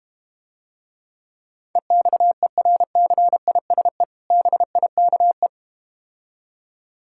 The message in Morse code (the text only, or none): EXERCISE BIKE